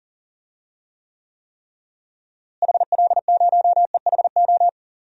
Morse code, heard as a keyed tone, about 40 wpm, 700 Hz. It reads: HL0EHO
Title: DXCC-100 Call Signs - 6 Characters - Repeated 3x - 40wpm